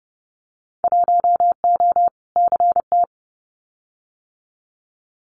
Morse code, heard as a keyed tone, about 30 wpm, 700 Hz. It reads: 1O CT
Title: Winter Field Day Exchanges Encore 30wpm